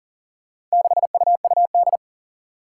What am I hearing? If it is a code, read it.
Morse code, heard as a keyed tone, about 40 wpm, 700 Hz. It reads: 6UUD
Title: Four Challenging Letter-Number Combinations 40wpm